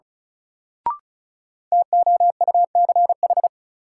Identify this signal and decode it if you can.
Morse code, heard as a keyed tone, about 36 wpm, 700 Hz. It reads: TOUCH
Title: Sentences from Top 1200 Words - New Word Drill - 35wpm